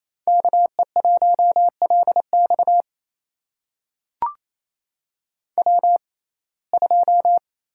Morse code, heard as a keyed tone, about 28 wpm, spaced as for 20 wpm, 700 Hz. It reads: KE1LX W2
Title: USA Call Signs - 5 Characters - 6x Character Spacing - 28wpm